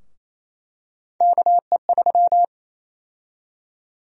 Morse code, heard as a keyed tone, about 28 wpm, 700 Hz. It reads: KE3
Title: USA Call Sign Prefixes - 28wpm